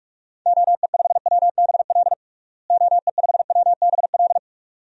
Morse code, heard as a keyed tone, about 48 wpm, 700 Hz. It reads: OE5WBL OE5WBL
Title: DXCC-100 Call Signs - 6 Characters - Repeated 2x - 45wpm